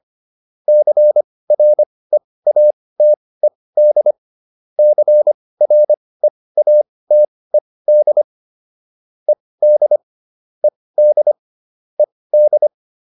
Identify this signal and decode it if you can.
Morse code, heard as a keyed tone, about 25 wpm, spaced as for 18 wpm, 600 Hz. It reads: CREATED CREATED ED ED ED
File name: ED7 18